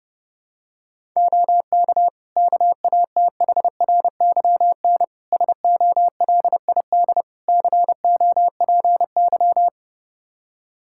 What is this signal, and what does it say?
Morse code, heard as a keyed tone, about 30 wpm, 700 Hz. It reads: OK KATHRYN SOLID COPY
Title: K1USN SST QSO Practice 30wpm